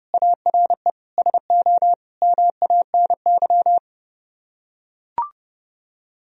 Morse code, heard as a keyed tone, about 31 wpm, 700 Hz. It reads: ARE SO MANY